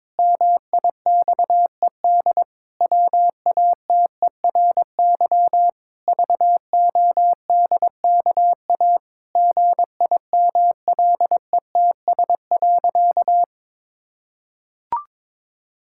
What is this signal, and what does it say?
Morse code, heard as a keyed tone, about 22 wpm, 700 Hz. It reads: MIXED WATERY VODKA GIMLETS.
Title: Pangram Challenge v2 - 22wpm